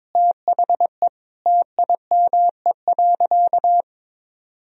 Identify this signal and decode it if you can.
Morse code, heard as a keyed tone, about 22 wpm, 700 Hz. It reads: THE TIME.
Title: Sentences from Top 200 Words and Abbreviations 22wpm